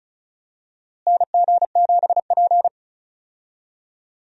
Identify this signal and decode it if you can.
Morse code, heard as a keyed tone, about 36 wpm, 700 Hz. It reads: NG7P
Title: USA Call Signs - 4 Characters - Repeated 2x - 35wpm